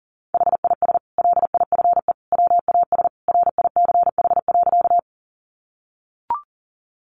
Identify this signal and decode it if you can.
Morse code, heard as a keyed tone, about 40 wpm, 700 Hz. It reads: HIS LIFE WAS RICH.
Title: Sentences from Top 1100 Words 40wpm